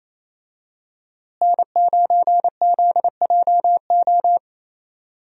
Morse code, heard as a keyed tone, about 27 wpm, 700 Hz. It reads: N9ZJO
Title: USA Call Signs - 5 Characters - Repeated 3x - 28wpm